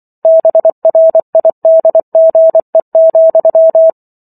Morse code, heard as a keyed tone, about 24 wpm, 650 Hz. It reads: BRIDGE,